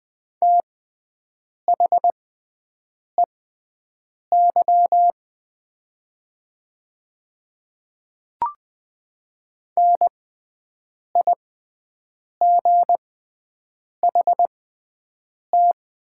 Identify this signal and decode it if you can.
Morse code, heard as a keyed tone, about 20 wpm, spaced as for 7 wpm, 700 Hz. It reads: THEY NIGHT
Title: 6x Character Spacing - Top 300 Words - 20wpm